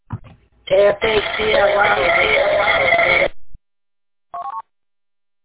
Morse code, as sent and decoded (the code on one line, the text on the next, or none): none